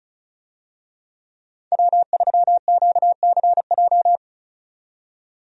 Morse code, heard as a keyed tone, about 34 wpm, 700 Hz. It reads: W3QCJ
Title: USA Call Signs - 5 Characters - 35wpm